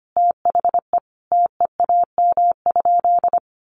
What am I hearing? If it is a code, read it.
Morse code, heard as a keyed tone, about 25 wpm, 700 Hz. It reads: THE TEAM?
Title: Sentences from Top 1500 Words 25wpm